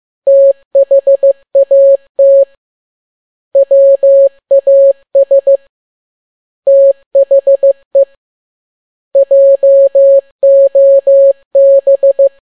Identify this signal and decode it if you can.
Morse code, heard as a keyed tone, about 15 wpm, 550 Hz. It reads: THAT WAS THE JOB
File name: n6tt p1 15C13